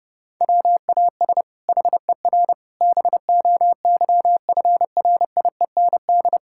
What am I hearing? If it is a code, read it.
Morse code, heard as a keyed tone, about 30 wpm, 700 Hz. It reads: WAS HER BOYFRIEND
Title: Sentences from Top 1000 Words - Review Part 5 - 30wpm